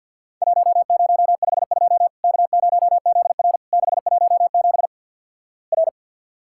Morse code, heard as a keyed tone, about 51 wpm, 700 Hz. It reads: 1052 K0ZR 617 R
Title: Worked All Europe DX Contest - QTC Exchange - 50wpm